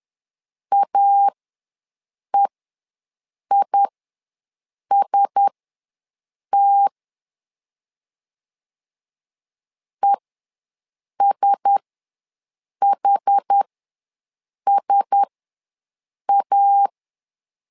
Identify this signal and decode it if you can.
Morse code, heard as a keyed tone, about 10 wpm, spaced as for 6 wpm, 800 Hz. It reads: AEIST ESHSA
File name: LEZIONI / 4 Q1S ws 10wpm 2208